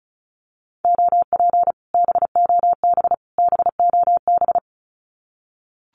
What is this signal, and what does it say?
Morse code, heard as a keyed tone, about 36 wpm, 700 Hz. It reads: OP BOB BOB
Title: Common QSO Phrases Encore 35wpm